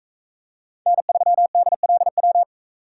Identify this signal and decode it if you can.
Morse code, heard as a keyed tone, about 40 wpm, 700 Hz. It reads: N3DLW